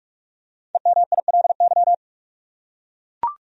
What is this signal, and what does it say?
Morse code, heard as a keyed tone, about 45 wpm, 700 Hz. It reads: EMILY
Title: USA Common Names - 45wpm